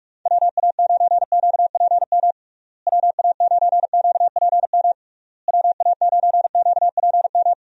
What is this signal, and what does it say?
Morse code, heard as a keyed tone, about 45 wpm, 700 Hz. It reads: WA9QPM WA9QPM WA9QPM
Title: USA Call Signs - 6 Characters - Repeated 3x - 45wpm